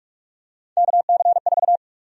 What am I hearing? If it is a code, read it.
Morse code, heard as a keyed tone, about 41 wpm, 700 Hz. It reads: KC4
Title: USA Call Sign Prefixes - Repeated 2x - 45wpm